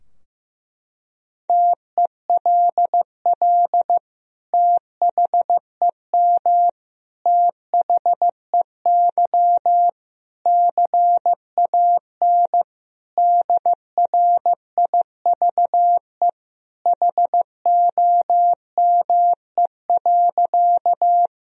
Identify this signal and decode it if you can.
Morse code, heard as a keyed tone, about 15 wpm, 700 Hz. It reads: TELL THEM THEY CAN DRIVE HOME.
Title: Sentences from Top 1000 Words - Review Part 2 - 15wpm